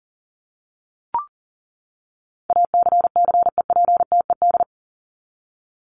Morse code, heard as a keyed tone, about 40 wpm, 700 Hz. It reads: ACCEPTED
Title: ED Suffix - Top 100 Words with Base Words 40wpm